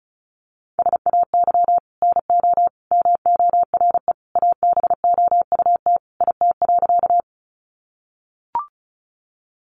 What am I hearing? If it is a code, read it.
Morse code, heard as a keyed tone, about 34 wpm, 700 Hz. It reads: SAY NO MORE ABOUT IT.
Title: Sentences from Top 200 Words 35wpm